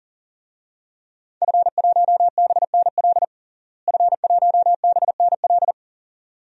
Morse code, heard as a keyed tone, about 40 wpm, 700 Hz. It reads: F1BNL F1BNL
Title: DXCC-100 Call Signs - 5 Characters - Repeated 2x - 40wpm